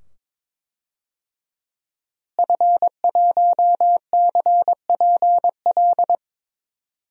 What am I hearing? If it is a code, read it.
Morse code, heard as a keyed tone, about 22 wpm, 700 Hz. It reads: F1CPL
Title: DXCC-100 Call Signs - 5 Characters - Repeated 2x - 22wpm